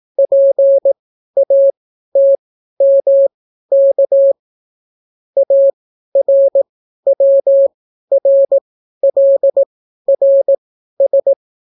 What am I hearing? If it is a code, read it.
Morse code, heard as a keyed tone, about 18 wpm, spaced as for 12 wpm, 550 Hz. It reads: PATMK ARWRLRS